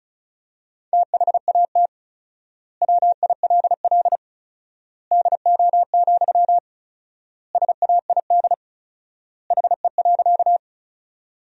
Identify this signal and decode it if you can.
Morse code, heard as a keyed tone, about 36 wpm, 700 Hz. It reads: THAT WILL DO, SAID HE.
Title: 4x Word-Spacing - Sentences from Top 100 Words - 35wpm